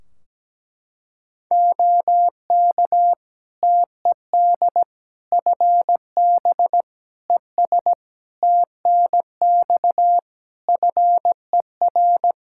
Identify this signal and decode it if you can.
Morse code, heard as a keyed tone, about 17 wpm, 700 Hz. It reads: OK TED FB ES TNX FER
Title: Practice QSO - Exchange 2 - Info - 17wpm